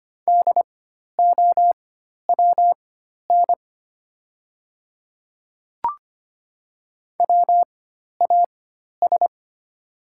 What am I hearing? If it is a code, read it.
Morse code, heard as a keyed tone, about 25 wpm, spaced as for 12 wpm, 700 Hz. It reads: DOWN WAS
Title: 4x Character Spacing - Top 500 Words - 25wpm